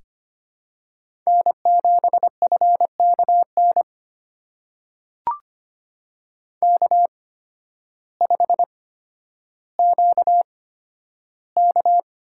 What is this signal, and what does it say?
Morse code, heard as a keyed tone, about 25 wpm, spaced as for 12 wpm, 700 Hz. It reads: N7FKN K5QK
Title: USA Call Signs - 8x Character Spacing - 25wpm